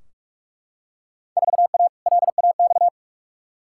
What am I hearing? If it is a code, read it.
Morse code, heard as a keyed tone, about 43 wpm, 700 Hz. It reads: 4A LAX
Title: Field Day Exchanges - 45wpm